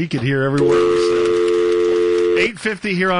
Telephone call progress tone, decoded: dial tone